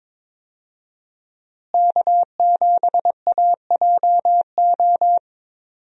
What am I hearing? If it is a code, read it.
Morse code, heard as a keyed tone, about 22 wpm, 700 Hz. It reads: K7AJO
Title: USA Call Signs - 5 Characters - Repeated 3x - 22wpm